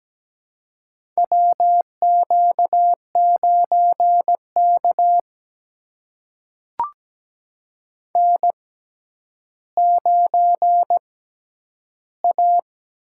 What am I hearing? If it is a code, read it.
Morse code, heard as a keyed tone, about 17 wpm, spaced as for 10 wpm, 700 Hz. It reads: WQ9K N9A
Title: USA Call Signs - 4 Characters - 6x Character Spacing - 17wpm